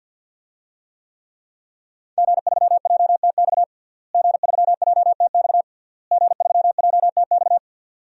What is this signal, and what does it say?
Morse code, heard as a keyed tone, about 51 wpm, 700 Hz. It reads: G3JTX G3JTX G3JTX
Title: DXCC-100 Call Signs - 5 Characters - Repeated 3x - 50wpm